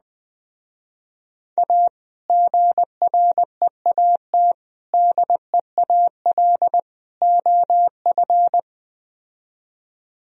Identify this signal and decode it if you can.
Morse code, heard as a keyed tone, about 20 wpm, 700 Hz. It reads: A GREAT DEAL OF
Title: Speed-Racing - Sets of 4 Words 17wpm